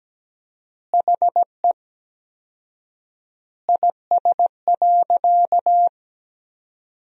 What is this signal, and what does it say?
Morse code, heard as a keyed tone, about 17 wpm, 700 Hz. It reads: HE IS.
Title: 4x Word-Spacing - Sentences from Top 200 Words - 17wpm